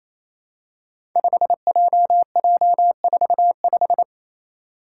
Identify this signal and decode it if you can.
Morse code, heard as a keyed tone, about 28 wpm, 700 Hz. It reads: 5JJ45
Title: Five Challenging Letter-Number Combinations 28wpm